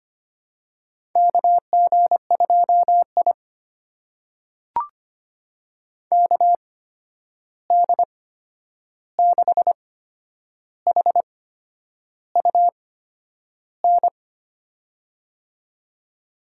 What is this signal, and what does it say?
Morse code, heard as a keyed tone, about 25 wpm, spaced as for 9 wpm, 700 Hz. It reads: KG2I KD6HUN
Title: USA Call Signs - 8x Character Spacing - 25wpm